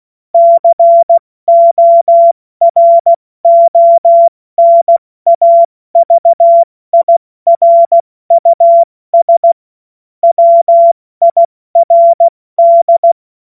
Morse code, heard as a keyed tone, about 16 wpm, 680 Hz. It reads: CORONAVIRUS WIRD
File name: Morsekurs / Witze / Witz 01. Juni 2021, 70 F80